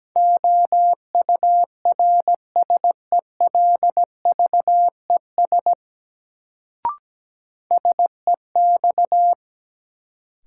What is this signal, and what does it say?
Morse code, heard as a keyed tone, about 17 wpm, 700 Hz. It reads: OURSELVES SEX
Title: Sentences from Top 1600 Words - New Word Drill Alt - 17wpm